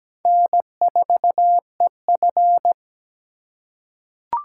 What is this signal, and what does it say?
Morse code, heard as a keyed tone, about 17 wpm, 700 Hz. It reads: N4EF
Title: USA Call Signs - 4 Characters - 2x Character Spacing - 17wpm